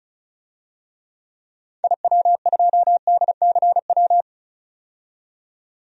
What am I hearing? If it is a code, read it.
Morse code, heard as a keyed tone, about 36 wpm, 700 Hz. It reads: IW2DCW
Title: DXCC-100 Call Signs - 6 Characters - 35wpm